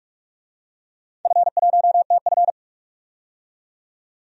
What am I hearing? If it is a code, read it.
Morse code, heard as a keyed tone, about 44 wpm, 690 Hz. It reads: F1TF
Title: DXCC-100 Call Signs - 4 Characters - 45wpm